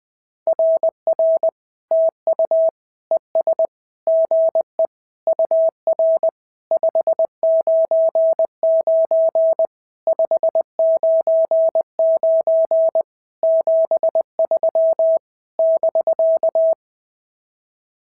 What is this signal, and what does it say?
Morse code, heard as a keyed tone, about 20 wpm, 650 Hz. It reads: RR TU ES GE UR 599 599 73 <BK>
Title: SOTA QSO Practice 20wpm